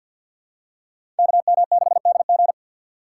Morse code, heard as a keyed tone, about 52 wpm, 700 Hz. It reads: KM6DG
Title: USA Call Signs - 5 Characters - 3x Character Spacing - 50wpm